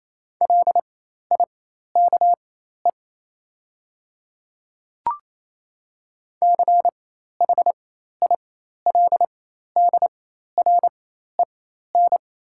Morse code, heard as a keyed tone, about 27 wpm, spaced as for 13 wpm, 690 Hz. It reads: LIKE CHILDREN
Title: 4x Character Spacing - Top 300 Words - 28wpm